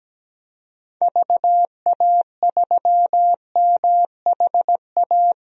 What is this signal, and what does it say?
Morse code, heard as a keyed tone, about 17 wpm, 700 Hz. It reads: VA3MHA
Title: DXCC-100 Call Signs - 6 Characters - Repeated 2x - 17wpm